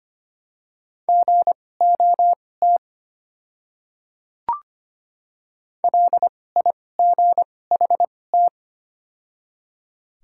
Morse code, heard as a keyed tone, about 25 wpm, spaced as for 18 wpm, 700 Hz. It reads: GOT LIGHT
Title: 2x Character Spacing - Top 300 Words - 25wpm